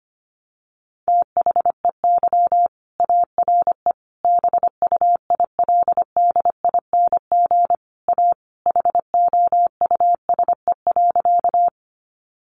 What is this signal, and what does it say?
Morse code, heard as a keyed tone, about 25 wpm, 700 Hz. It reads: THEY ARE BUILDING A HOUSE.
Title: Sentences from Top 700 Words 25wpm